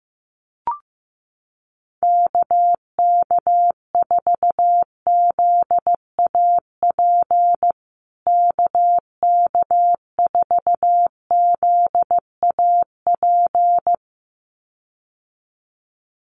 Morse code, heard as a keyed tone, about 15 wpm, 700 Hz. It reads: KK4ZAP KK4ZAP
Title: USA Call Signs Repeated 2x 15wpm